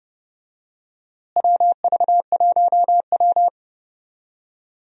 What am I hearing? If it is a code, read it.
Morse code, heard as a keyed tone, about 31 wpm, 700 Hz. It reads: WV1W
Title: USA Call Signs - 4 Characters - Repeated 3x - 30wpm